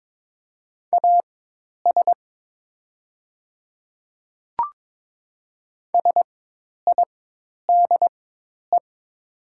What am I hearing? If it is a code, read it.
Morse code, heard as a keyed tone, about 22 wpm, spaced as for 10 wpm, 700 Hz. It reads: AS SIDE
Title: 4x Character Spacing - Top 100 Words - 22wpm